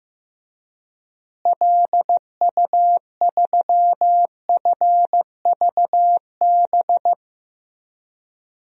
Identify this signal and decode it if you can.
Morse code, heard as a keyed tone, about 15 wpm, 700 Hz. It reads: LU3FVB
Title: DXCC-100 Call Signs - 6 Characters - Repeated 3x - 15wpm